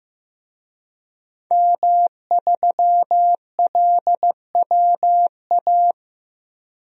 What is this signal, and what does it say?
Morse code, heard as a keyed tone, about 15 wpm, 700 Hz. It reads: M3LWA
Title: DXCC-100 Call Signs - 5 Characters - Repeated 3x - 15wpm